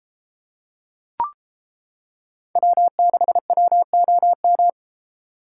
Morse code, heard as a keyed tone, about 30 wpm, 700 Hz. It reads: W6WOM